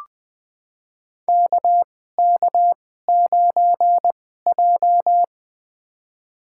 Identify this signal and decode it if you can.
Morse code, heard as a keyed tone, about 21 wpm, spaced as for 15 wpm, 700 Hz. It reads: KK9J
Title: USA Call Signs - 4 Characters - 2x Character Spacing - 20wpm